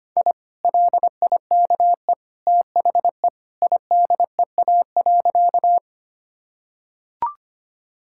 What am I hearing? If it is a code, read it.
Morse code, heard as a keyed tone, about 25 wpm, 700 Hz. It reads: I LIKE THE IDEA.
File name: practice / 25wpm / Sentences with 2-Word Phrases - 25wpm